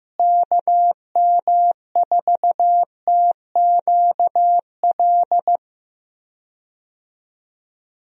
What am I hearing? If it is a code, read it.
Morse code, heard as a keyed tone, about 15 wpm, 700 Hz. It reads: KM4TQL
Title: USA Call Signs - 6 Characters - Repeated 3x - 15wpm